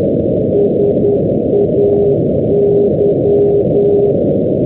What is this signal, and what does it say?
Morse code, heard as a keyed tone, about 10 wpm, 410 Hz, with noise. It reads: SAY
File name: europe / g-625 ndb stornoway-hebrides say-431-np-g-shetlands-2012-04-rb